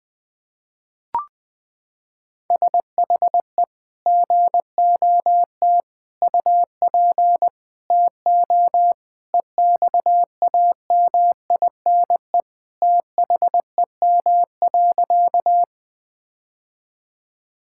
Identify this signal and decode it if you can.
Morse code, heard as a keyed tone, about 20 wpm, 700 Hz. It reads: SHE GOT UP TO EXAMINE THEM.
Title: Sentences from Top 1600 Words 20wpm